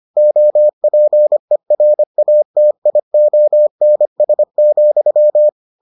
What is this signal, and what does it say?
Morse code, heard as a keyed tone, about 25 wpm, 600 Hz. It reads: OPERATIONS,